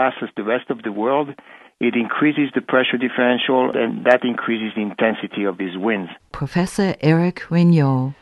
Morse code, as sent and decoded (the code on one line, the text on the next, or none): none